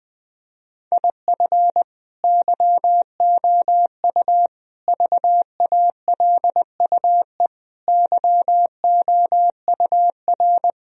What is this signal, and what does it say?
Morse code, heard as a keyed tone, about 20 wpm, 700 Hz. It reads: IF YOU VALUE YOUR
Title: Sentences from Top 1000 Words - Review Part 3 - 20wpm